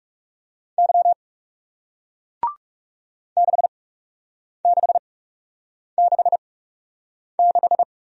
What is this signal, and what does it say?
Morse code, heard as a keyed tone, about 40 wpm, spaced as for 8 wpm, 700 Hz. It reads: Y 6666